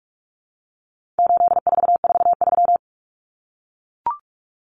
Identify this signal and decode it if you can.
Morse code, heard as a keyed tone, about 44 wpm, 700 Hz. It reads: 8443
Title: Common Cut Numbers - 4-Digit 45wpm